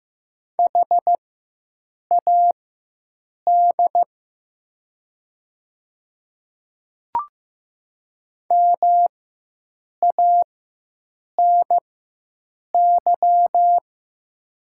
Morse code, heard as a keyed tone, about 15 wpm, spaced as for 7 wpm, 700 Hz. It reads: HAD MANY
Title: 4x Character Spacing - Top 100 Words - 15wpm